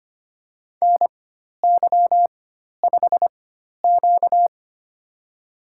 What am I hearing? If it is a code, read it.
Morse code, heard as a keyed tone, about 25 wpm, spaced as for 12 wpm, 700 Hz. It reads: NY5Q
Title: USA Call Signs - 4 Characters - 4x Character Spacing - 25wpm